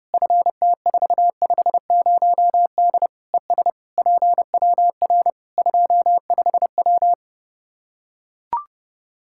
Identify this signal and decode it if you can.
Morse code, heard as a keyed tone, about 32 wpm, 700 Hz. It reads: FT450D ES PWR 25W
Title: Practice QSO - Exchange 2 - Info - 30wpm